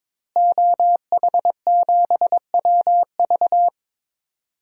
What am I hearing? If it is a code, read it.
Morse code, heard as a keyed tone, about 22 wpm, 700 Hz. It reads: OH7WV
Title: DXCC-100 Call Signs - 5 Characters - Repeated 3x - 22wpm